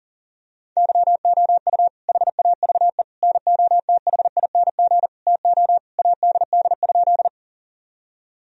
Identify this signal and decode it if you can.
Morse code, heard as a keyed tone, about 40 wpm, 700 Hz. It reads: YOU HAVE NOTHING TO ADD?
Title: Sentences from Top 1000 Words - Review Part 1 - 40wpm